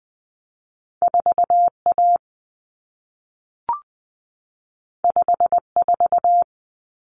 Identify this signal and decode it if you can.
Morse code, heard as a keyed tone, about 20 wpm, 700 Hz. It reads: 4A 54
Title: Common Cut Numbers - 2-Digit 20wpm